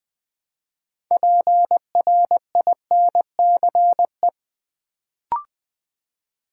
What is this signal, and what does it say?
Morse code, heard as a keyed tone, about 20 wpm, 700 Hz. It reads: PRINCE
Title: Build A Word - 3 Letters At A Time - 6-Letter Words - 20wpm